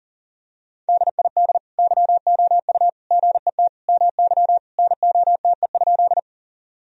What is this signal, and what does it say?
Morse code, heard as a keyed tone, about 40 wpm, 700 Hz. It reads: DID YOU GET MY NOTE?
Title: Sentences from Top 1000 Words - Review Part 2 - 40wpm